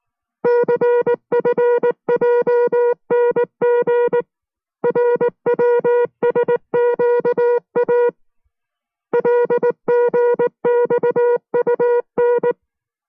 Morse code, heard as a keyed tone, about 19 wpm, 480 Hz. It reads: CFJNG RWSQA LGXUN